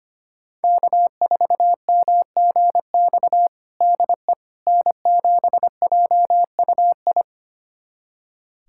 Morse code, heard as a keyed tone, about 25 wpm, 700 Hz. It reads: K4MGX DE N7JUI
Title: USA Callsign DE Callsign - 5 Characters - 25wpm